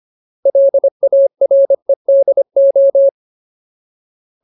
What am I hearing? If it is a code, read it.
Morse code, heard as a keyed tone, about 25 wpm, 550 Hz. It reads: LAREDO
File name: Ultimate Word List / 25WPM Ultimate